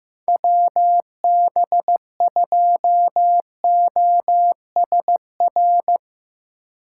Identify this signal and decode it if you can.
Morse code, heard as a keyed tone, about 15 wpm, 700 Hz. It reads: WB2OSR
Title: Speed Racing - USA 2x3 Call Signs 15wpm